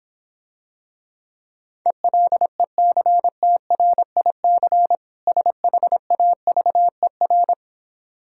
Morse code, heard as a keyed tone, about 26 wpm, 700 Hz. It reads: ELECTRIC SHAVER